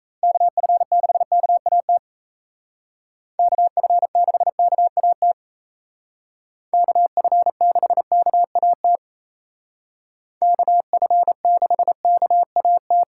Morse code, heard as a keyed tone, about 36 wpm, 700 Hz. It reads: KF6KAT KF6KAT KF6KAT KF6KAT